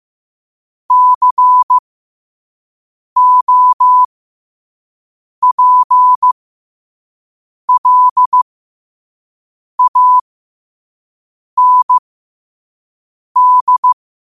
Morse code, heard as a keyed tone, about 15 wpm, spaced as for 5 wpm, 1000 Hz. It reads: COPLAND
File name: 5wpm